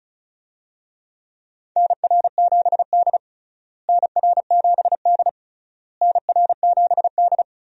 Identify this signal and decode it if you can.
Morse code, heard as a keyed tone, about 36 wpm, 700 Hz. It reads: NR7D NR7D NR7D
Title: USA Call Signs - 4 Characters - Repeated 3x - 35wpm